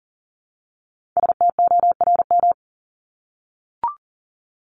Morse code, heard as a keyed tone, about 40 wpm, 700 Hz. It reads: STORM
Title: Top 1251-1500 Words 40wpm